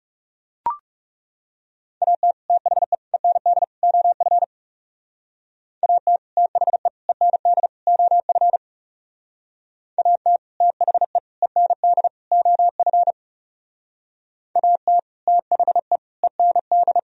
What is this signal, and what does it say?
Morse code, heard as a keyed tone, about 38 wpm, 700 Hz. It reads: AT THE END OF AT THE END OF AT THE END OF AT THE END